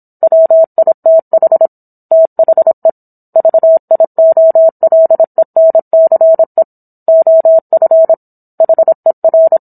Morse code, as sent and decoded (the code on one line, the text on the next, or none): .-- .. - .... / - .... / ...- .. --- .-.. -. -.-. / --- ..-. / .... .-.
WITH THE VIOLENCE OF HER